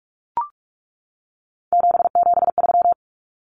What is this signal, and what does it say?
Morse code, heard as a keyed tone, about 44 wpm, 700 Hz. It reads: 773
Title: Common Cut Numbers - 3-Digit 45wpm